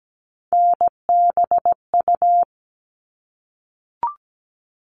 Morse code, heard as a keyed tone, about 17 wpm, 700 Hz. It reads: NBU
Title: All Cut Numbers - 3-Digit 17wpm